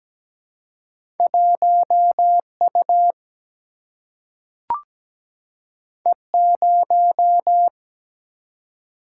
Morse code, heard as a keyed tone, about 17 wpm, 700 Hz. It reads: 1 U E 0
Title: Single-Letter Single-Number 17wpm